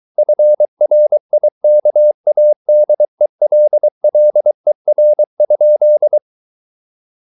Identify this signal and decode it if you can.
Morse code, heard as a keyed tone, about 23 wpm, 600 Hz. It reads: FRIKADELLER?
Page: Ferie 23 wpm